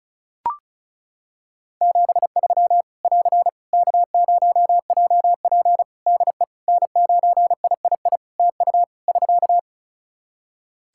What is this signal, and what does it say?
Morse code, heard as a keyed tone, about 34 wpm, 700 Hz. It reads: 73 <AR> K0JP DE N9III TU <SK>
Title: Practice QSO - Exchange 4 - Ending - 35wpm